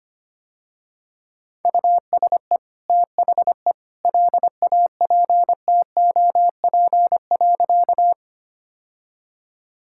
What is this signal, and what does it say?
Morse code, heard as a keyed tone, about 25 wpm, 700 Hz. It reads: USE THE LAPTOP.
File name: Easy Sentences with Hints - 25wpm